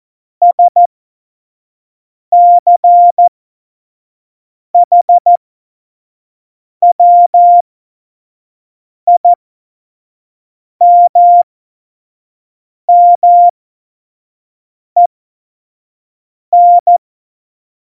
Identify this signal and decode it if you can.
Morse code, heard as a keyed tone, about 14 wpm, spaced as for 5 wpm, 700 Hz. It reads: SCHWIMMEN
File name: Morsekurs / Witze / Witz 19. November 2019, 25 F70